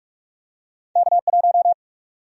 Morse code, heard as a keyed tone, about 44 wpm, 700 Hz. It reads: K1
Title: USA Call Sign Prefixes - 45wpm